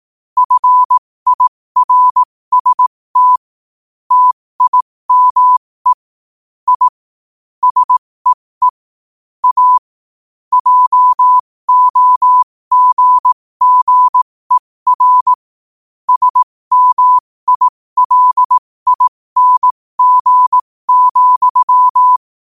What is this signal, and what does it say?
Morse code, heard as a keyed tone, about 18 wpm, spaced as for 15 wpm, 1000 Hz. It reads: FIRST TIME I SEE A JOGGER SMILING,
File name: qotdfiles / 15wpm